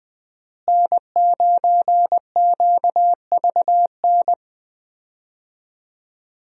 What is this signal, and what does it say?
Morse code, heard as a keyed tone, about 20 wpm, 700 Hz. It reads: N9QVN
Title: USA Call Signs - 5 Characters - Repeated 2x - 20wpm